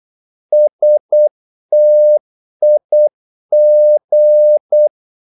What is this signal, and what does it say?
Morse code, heard as a keyed tone, about 8 wpm, 600 Hz. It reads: STIG
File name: Korte ord DK 8wpm